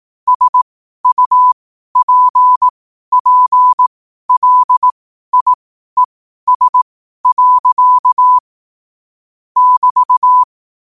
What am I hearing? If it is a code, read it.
Morse code, heard as a keyed tone, about 18 wpm, spaced as for 12 wpm, 1000 Hz. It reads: SUPPLIES. <BT>
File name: qotdfiles / 12wpm